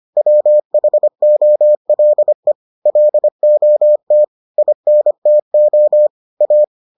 Morse code, heard as a keyed tone, about 25 wpm, 600 Hz. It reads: WHOLE LOT INTO A